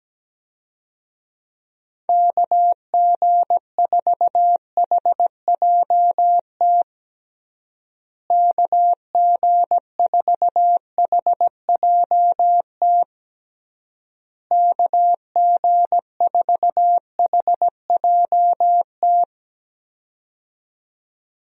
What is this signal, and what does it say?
Morse code, heard as a keyed tone, about 17 wpm, 700 Hz. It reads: KG4HJT KG4HJT KG4HJT
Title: USA Call Signs - 6 Characters - Repeated 3x - 17wpm